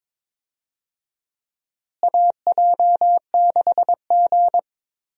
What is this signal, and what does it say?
Morse code, heard as a keyed tone, about 22 wpm, 700 Hz. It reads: AJ6G
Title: USA Call Signs - 4 Characters - Repeated 2x - 22wpm